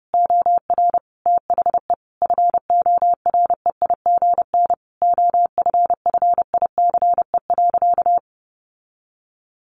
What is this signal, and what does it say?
Morse code, heard as a keyed tone, about 30 wpm, 700 Hz. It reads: OR THE FOREIGN OFFICE.
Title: Sentences from Top 1400 Words 30wpm